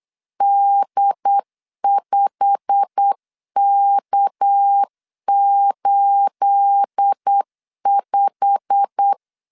Morse code, heard as a keyed tone, about 8 wpm, 800 Hz. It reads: D5K85
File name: LEZIONI / 9 28 8wpm 3512